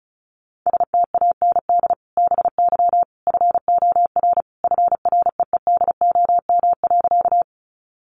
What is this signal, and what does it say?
Morse code, heard as a keyed tone, about 34 wpm, 700 Hz. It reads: STAND BY FOR FREEDOM.
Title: Sentences from Top 800 Words 35wpm